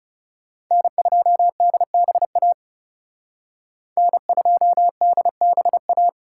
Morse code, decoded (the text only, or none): N2DBA N2DBA